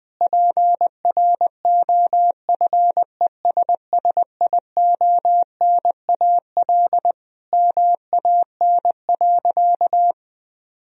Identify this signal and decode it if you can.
Morse code, heard as a keyed tone, about 20 wpm, 700 Hz. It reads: PROFESSIONAL MAN.
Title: Sentences from Top 1000 Words - Review Part 4 - 20wpm